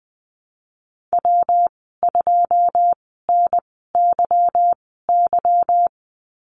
Morse code, heard as a keyed tone, about 20 wpm, spaced as for 15 wpm, 700 Hz. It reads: W2NYY
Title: 167 Lesson 27g - Number 7 - Call Signs 20wpm